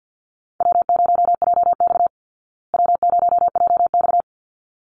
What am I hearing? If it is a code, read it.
Morse code, heard as a keyed tone, about 48 wpm, 700 Hz. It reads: W0JX W0JX
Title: Call Signs Repeated 3x - 50wpm